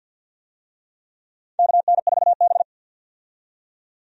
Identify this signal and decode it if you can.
Morse code, heard as a keyed tone, about 48 wpm, 700 Hz. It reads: KN4B
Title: USA Call Signs - 4 Characters - 50wpm